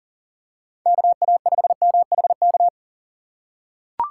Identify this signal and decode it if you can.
Morse code, heard as a keyed tone, about 40 wpm, 700 Hz. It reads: KA5MHK
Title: USA Call Signs - 6 Characters - 4x Character Spacing - 40wpm